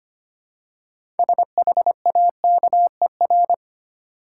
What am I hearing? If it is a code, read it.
Morse code, heard as a keyed tone, about 26 wpm, 700 Hz. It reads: SHAKER
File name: practice / 25wpm / Easy Words with Hints - 25wpm